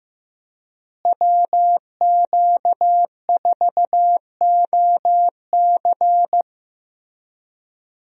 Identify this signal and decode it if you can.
Morse code, heard as a keyed tone, about 15 wpm, 700 Hz. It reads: WQ4OC
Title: USA Call Signs - 5 Characters - Repeated 3x - 15wpm